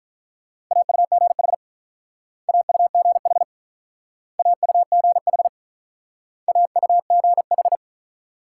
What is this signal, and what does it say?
Morse code, heard as a keyed tone, about 46 wpm, 700 Hz. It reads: AUGH AUGH AUGH AUGH